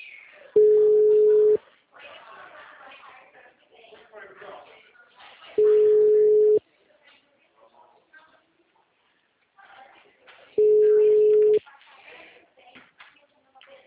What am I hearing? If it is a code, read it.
Ringback tone (Europe, 425 Hz), with speech.